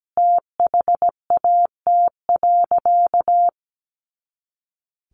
Morse code, heard as a keyed tone, about 17 wpm, 700 Hz. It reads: THAT.
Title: Sentences from Top 500 Words 17wpm